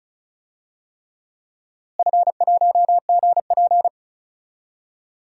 Morse code, heard as a keyed tone, about 36 wpm, 700 Hz. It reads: F1GP
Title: DXCC-100 Call Signs - 4 Characters - Repeated 3x - 35wpm